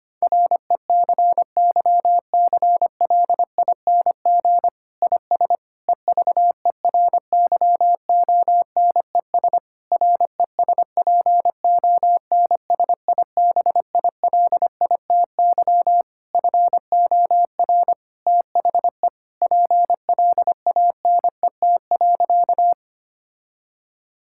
Morse code, heard as a keyed tone, about 25 wpm, 700 Hz. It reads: RECYCLING IS EVERYONES RESPONSIBILITY FOR THE PLANET.
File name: practice / 25wpm / Difficult Sentences with Hints - 25wpm